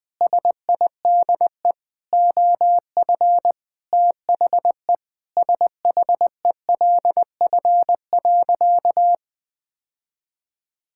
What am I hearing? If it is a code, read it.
Morse code, heard as a keyed tone, about 20 wpm, 700 Hz. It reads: SIDE OF THE SHELF.
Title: Sentences with 7-Word Phrases - 20wpm